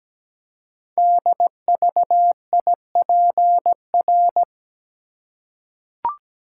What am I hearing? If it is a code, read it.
Morse code, heard as a keyed tone, about 17 wpm, 700 Hz. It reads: DVIPR
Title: Code Groups - 5 Letters - 17wpm